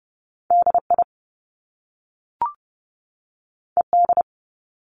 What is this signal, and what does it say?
Morse code, heard as a keyed tone, about 30 wpm, 700 Hz. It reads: DI ED
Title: Top Two Letter Combinations in Words 30wpm